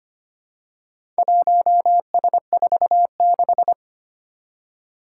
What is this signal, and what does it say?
Morse code, heard as a keyed tone, about 25 wpm, 700 Hz. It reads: 1S46
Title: Four Challenging Letter-Number Combinations 25wpm